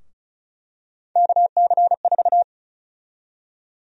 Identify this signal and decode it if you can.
Morse code, heard as a keyed tone, about 34 wpm, 700 Hz. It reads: KC4
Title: USA Call Sign Prefixes - 35wpm